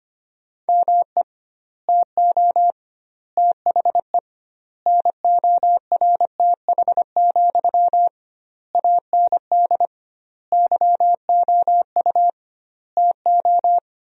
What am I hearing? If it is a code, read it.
Morse code, heard as a keyed tone, about 25 wpm, 700 Hz. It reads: ME TO THE NORTH, AND YOU TO
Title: 2x Word-Spacing - Sentences from Top 400 Words - 25wpm